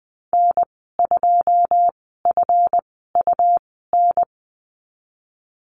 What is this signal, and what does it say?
Morse code, heard as a keyed tone, about 20 wpm, spaced as for 14 wpm, 700 Hz. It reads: N2FUN
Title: 173 Lesson 28g - Number 9 - Call Signs 20wpm